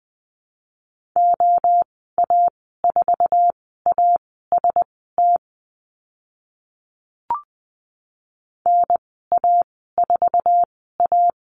Morse code, heard as a keyed tone, about 20 wpm, spaced as for 14 wpm, 700 Hz. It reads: OA4AST NA4A
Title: 048 Lesson 9g - Number 4 - Call Signs 20wpm